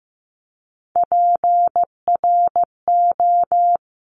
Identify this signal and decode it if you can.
Morse code, heard as a keyed tone, about 15 wpm, 700 Hz. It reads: PRO
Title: Top Three Letter Combinations in Words 15wpm